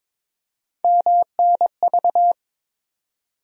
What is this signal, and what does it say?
Morse code, heard as a keyed tone, about 21 wpm, 700 Hz. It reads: MNV
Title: USA Call Sign Postfixes - 3 Letters 22wpm